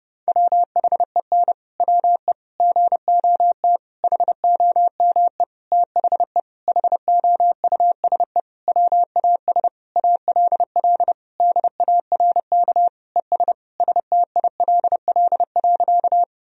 Morse code, heard as a keyed tone, about 32 wpm, 700 Hz. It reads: WHEN WE GOT HOME THE HOUSE WAS ALL DARK ES STILL.
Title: Sentences from Top 500 Words and Abbreviations 30wpm